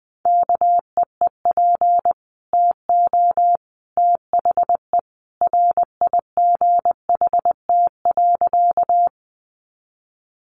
Morse code, heard as a keyed tone, about 20 wpm, 700 Hz. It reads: KEEP TO THE RIGHT.
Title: Sentences from Top 300 Words 20wpm